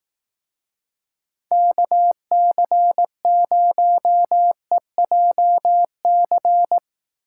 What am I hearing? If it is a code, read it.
Morse code, heard as a keyed tone, about 18 wpm, 700 Hz. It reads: KC0EJC
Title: Speed Racing - USA 2x3 Call Signs 15wpm